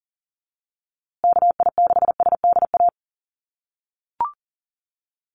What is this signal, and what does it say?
Morse code, heard as a keyed tone, about 40 wpm, 700 Hz. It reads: KI6SDA
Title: USA Call Signs Repeated 3x 40wpm